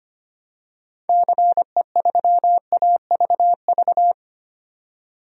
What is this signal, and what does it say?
Morse code, heard as a keyed tone, about 25 wpm, 700 Hz. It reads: CE3AVV
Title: DXCC-100 Call Signs - 6 Characters - 25wpm